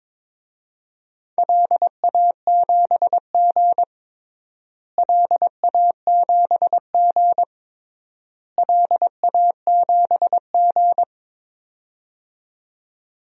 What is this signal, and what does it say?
Morse code, heard as a keyed tone, about 21 wpm, 700 Hz. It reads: LA7G LA7G LA7G
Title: DXCC-100 Call Signs - 4 Characters - Repeated 3x - 22wpm